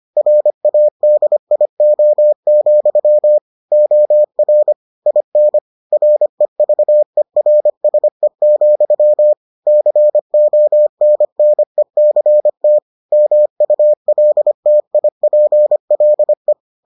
Morse code, heard as a keyed tone, about 25 wpm, 600 Hz. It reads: RADIO, OR IN REVERSE, CONNECT MULTIPLE